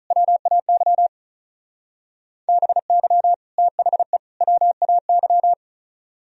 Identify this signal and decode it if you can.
Morse code, heard as a keyed tone, about 43 wpm, 700 Hz. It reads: WAY BY THE WAY
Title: Speed-Racing - Sets of 3 Words 35wpm